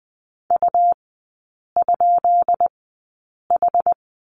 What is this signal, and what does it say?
Morse code, heard as a keyed tone, about 20 wpm, spaced as for 8 wpm, 690 Hz. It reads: U?H
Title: 159 Lesson 26e - Letter g - Three Characters 20wpm